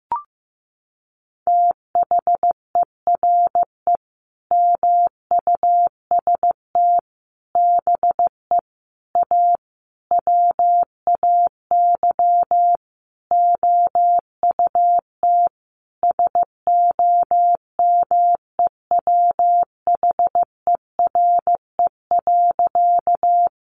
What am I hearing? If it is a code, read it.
Morse code, heard as a keyed tone, about 15 wpm, 700 Hz. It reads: THERE MUST BE A WAY OUT SOMEWHERE.
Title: Sentences from Top 1400 Words 15wpm